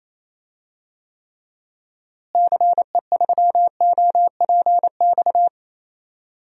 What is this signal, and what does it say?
Morse code, heard as a keyed tone, about 28 wpm, 700 Hz. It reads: CE3OPX
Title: DXCC-100 Call Signs - 6 Characters - Repeated 2x - 28wpm